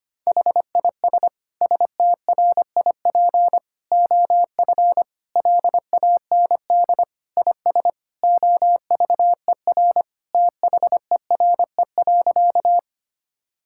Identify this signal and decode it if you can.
Morse code, heard as a keyed tone, about 25 wpm, 700 Hz. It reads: HIS STRIP OF LAND IS OVER THERE.
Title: Sentences from Top 1600 Words 25wpm